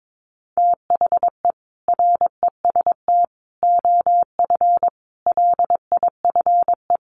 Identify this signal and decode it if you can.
Morse code, heard as a keyed tone, about 22 wpm, 700 Hz. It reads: THE REST OF LIFE
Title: Sentences from Top 1500 Words 22wpm